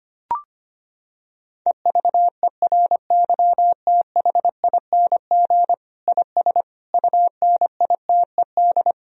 Morse code, heard as a keyed tone, about 25 wpm, 700 Hz. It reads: EVERYTHING IS UNITED
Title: Sentences from Top 1000 Words - Review Part 4 - 25wpm